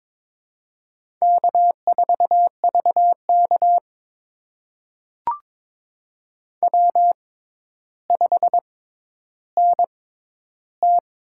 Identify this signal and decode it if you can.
Morse code, heard as a keyed tone, about 22 wpm, spaced as for 11 wpm, 700 Hz. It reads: K4VK W5NT
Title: USA Call Signs - 4 Characters - 6x Character Spacing - 22wpm